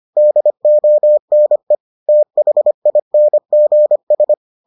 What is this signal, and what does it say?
Morse code, heard as a keyed tone, about 25 wpm, 600 Hz. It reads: DONE THINGS